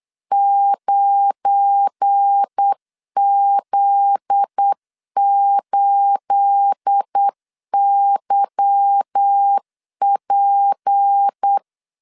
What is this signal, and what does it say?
Morse code, heard as a keyed tone, about 8 wpm, 800 Hz. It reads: 9Z8YP
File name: LEZIONI / 10 SIGNS / 10B INT 8wpm 3534